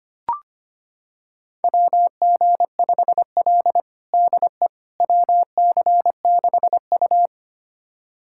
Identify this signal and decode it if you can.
Morse code, heard as a keyed tone, about 25 wpm, 700 Hz. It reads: WG5L DE WC6U
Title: USA Callsign DE Callsign - 4 Characters - 25wpm